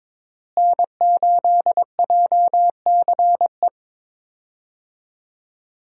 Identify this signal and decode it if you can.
Morse code, heard as a keyed tone, about 22 wpm, 690 Hz. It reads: N8JCE
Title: USA Call Signs - 5 Characters - Repeated 3x - 22wpm